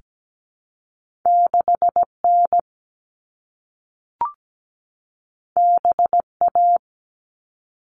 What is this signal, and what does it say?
Morse code, heard as a keyed tone, about 17 wpm, 700 Hz. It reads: 6N BA
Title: All Cut Numbers - 2-Digit 17wpm